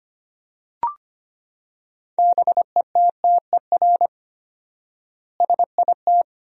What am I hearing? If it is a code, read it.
Morse code, heard as a keyed tone, about 25 wpm, 700 Hz. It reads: BETTER SIT
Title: 4x Word-Spacing - Sentences from Top 500 Words - 25wpm